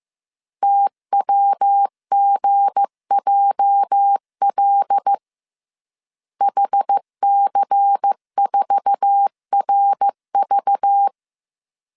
Morse code, heard as a keyed tone, about 15 wpm, 800 Hz. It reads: TWGJL HC4RV
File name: CW / DET / 1 9 1500C S A 14wpm 3222